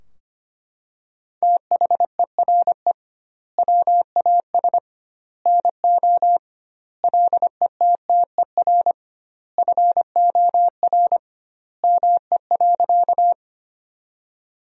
Morse code, heard as a keyed tone, about 25 wpm, 700 Hz. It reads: THERE WAS NO LETTER FOR ME.
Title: 2x Word-Spacing - Sentences from Top 300 Words - 25wpm